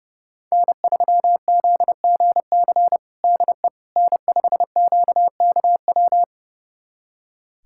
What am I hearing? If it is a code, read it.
Morse code, heard as a keyed tone, about 30 wpm, 700 Hz. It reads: N3ZGC DE N5QKW
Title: USA Callsign DE Callsign - 5 Characters - 30wpm